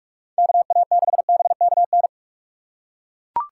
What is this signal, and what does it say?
Morse code, heard as a keyed tone, about 42 wpm, 700 Hz. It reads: KA6BKN
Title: USA Call Signs - 6 Characters - 6x Character Spacing - 45wpm